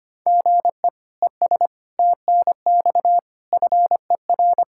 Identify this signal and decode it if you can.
Morse code, heard as a keyed tone, about 25 wpm, 700 Hz. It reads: GE ES TNX FER
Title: Practice QSO - Exchange 1 - Beginning - 25wpm